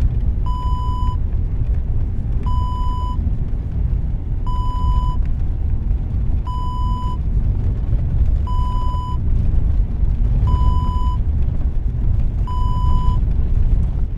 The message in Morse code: none